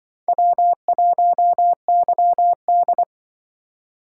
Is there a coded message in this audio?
Yes: Morse code from 0.3 to 3.0 s, heard as a keyed tone, about 26 wpm, 700 Hz. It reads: W1YD